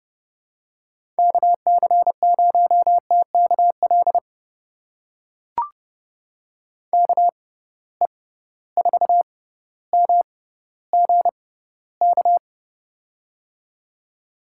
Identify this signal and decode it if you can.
Morse code, heard as a keyed tone, about 30 wpm, spaced as for 15 wpm, 700 Hz. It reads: KC0TKL KE4MGK
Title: USA Call Signs - 6 Characters - 6x Character Spacing - 30wpm